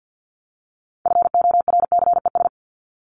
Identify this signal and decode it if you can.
Morse code, heard as a keyed tone, about 50 wpm, 700 Hz. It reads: FORCES
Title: ES:S Suffix - Top 100 Words with Base Words 50wpm